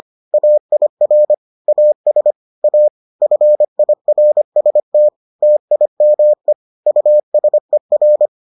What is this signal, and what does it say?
Morse code, heard as a keyed tone, about 25 wpm, 600 Hz. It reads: AIR AS A FIRST TIME USER